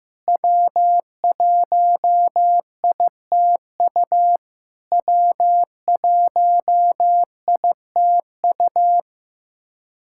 Morse code, heard as a keyed tone, about 15 wpm, 690 Hz. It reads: W1ITU W1ITU
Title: SOTA QSO Practice 15wpm